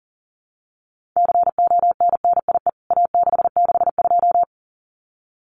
Morse code, heard as a keyed tone, about 40 wpm, 700 Hz. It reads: CONNIE A662
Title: CWT Exchange 40wpm